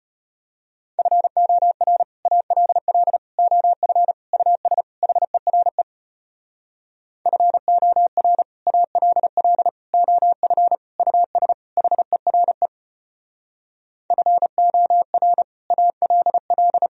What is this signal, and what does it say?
Morse code, heard as a keyed tone, about 33 wpm, 700 Hz. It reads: FOR ALL OF US HERE FOR ALL OF US HERE FOR ALL